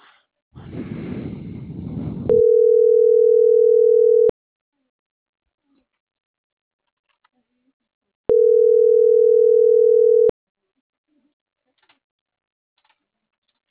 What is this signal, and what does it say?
Ringback tone (North America), 440 + 480 Hz.